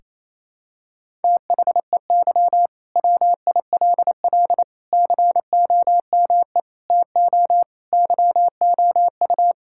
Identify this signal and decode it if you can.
Morse code, heard as a keyed tone, about 28 wpm, 700 Hz. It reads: THEY WILL COME TO YOU
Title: Sentences from Top 1600 Words 28wpm